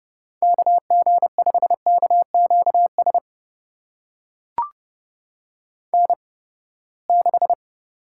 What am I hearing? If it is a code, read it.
Morse code, heard as a keyed tone, about 30 wpm, spaced as for 21 wpm, 700 Hz. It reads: KG5KQS N6
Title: USA Call Signs - 8x Character Spacing - 30wpm